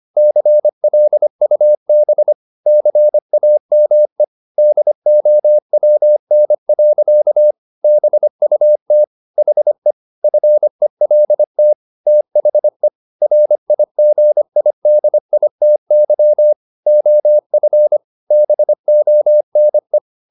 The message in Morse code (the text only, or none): CLUB CAME DOWN. BUT HE FELT THE RIGIDITY OF BONE